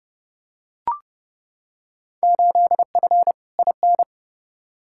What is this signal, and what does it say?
Morse code, heard as a keyed tone, about 30 wpm, 700 Hz. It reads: 8F IN
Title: Field Day Exchanges Encore - 30wpm